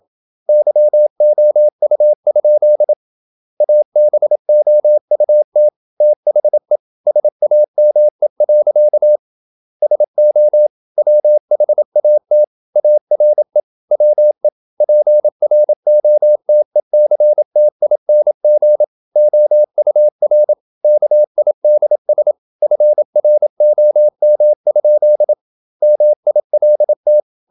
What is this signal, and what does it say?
Morse code, heard as a keyed tone, about 27 wpm, 600 Hz. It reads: YOU? ABOUT THE SAME. SO WHAT ARE WE PROTECTING OUR KIDS FROM? MILT
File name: Never 0027wpm